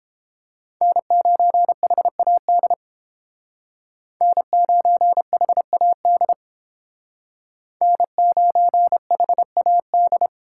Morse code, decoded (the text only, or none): N9HAD N9HAD N9HAD